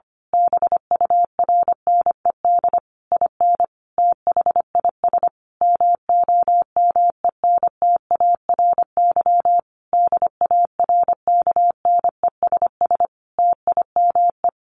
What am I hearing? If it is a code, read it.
Morse code, heard as a keyed tone, about 25 wpm, 700 Hz. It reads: BURNED IN THIS MOMENTARY DARKNESS TIME
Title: A Sound Of Thunder 25wpm